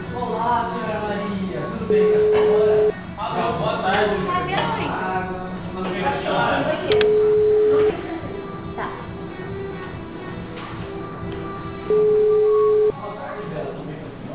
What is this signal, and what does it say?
Ringback tone (Europe, 425 Hz), with speech.